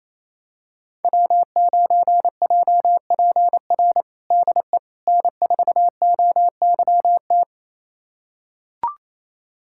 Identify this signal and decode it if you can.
Morse code, heard as a keyed tone, about 27 wpm, 700 Hz. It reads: W9JPR DE N4OYT
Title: USA Callsign DE Callsign - 5 Characters - 28wpm